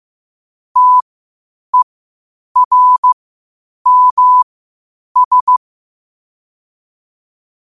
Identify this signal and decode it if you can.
Morse code, heard as a keyed tone, about 15 wpm, spaced as for 8 wpm, 1000 Hz. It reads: TERMS
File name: qotdfiles / 8wpm